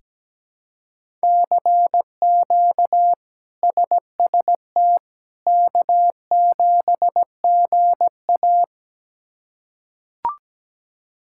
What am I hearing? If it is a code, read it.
Morse code, heard as a keyed tone, about 17 wpm, 700 Hz. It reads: CQ SST K7GA
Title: K1USN SST QSO Practice - 17wpm